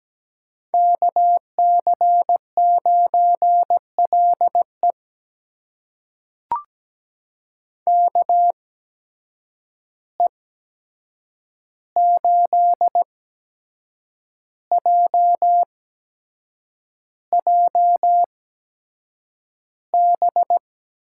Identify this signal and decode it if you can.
Morse code, heard as a keyed tone, about 17 wpm, spaced as for 7 wpm, 700 Hz. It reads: KC9LE KE8JJB
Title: USA Call Signs - 8x Character Spacing - 17wpm